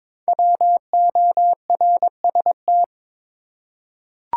Morse code, heard as a keyed tone, about 22 wpm, 700 Hz. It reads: WORST
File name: 5 Letter Words - First 3 Letters Given - 22wpm